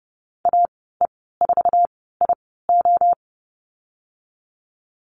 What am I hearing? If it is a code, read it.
Morse code, heard as a keyed tone, about 30 wpm, spaced as for 17 wpm, 700 Hz. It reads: AE4IO
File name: Lesson 14g - Number 2 - Call Signs 30wpm